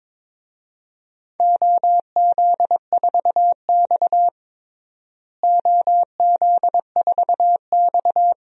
Morse code, heard as a keyed tone, about 22 wpm, 700 Hz. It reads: OZ4X OZ4X
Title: DXCC-100 Call Signs - 4 Characters - Repeated 3x - 22wpm